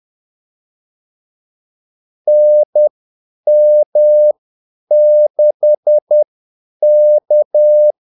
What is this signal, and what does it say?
Morse code, heard as a keyed tone, about 10 wpm, spaced as for 8 wpm, 600 Hz. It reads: NM6K
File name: Callsigns-10-8wpm